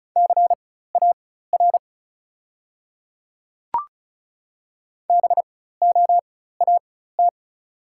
Morse code, heard as a keyed tone, about 36 wpm, spaced as for 16 wpm, 690 Hz. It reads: CAR BOAT
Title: 4x Character Spacing - Top 500 Words - 35wpm